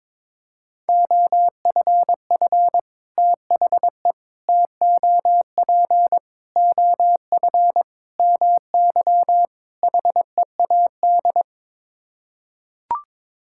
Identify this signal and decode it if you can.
Morse code, heard as a keyed tone, about 22 wpm, 700 Hz. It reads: OFF THE TOP OF MY HEAD
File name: practice / 22wpm / Sets of 6 Words - 22wpm